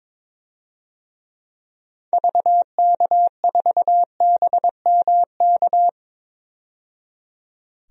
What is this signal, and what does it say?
Morse code, heard as a keyed tone, about 22 wpm, 700 Hz. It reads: VK4BMK
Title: DXCC-100 Call Signs - 6 Characters - 22wpm